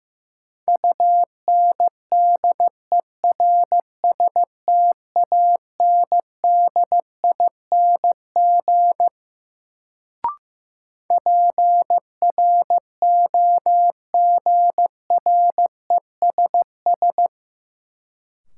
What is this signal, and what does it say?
Morse code, heard as a keyed tone, about 15 wpm, 700 Hz. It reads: UNDERSTANDING PROGRESS
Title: Sentences from Top 1400 Words - New Word Drill Alt - 15wpm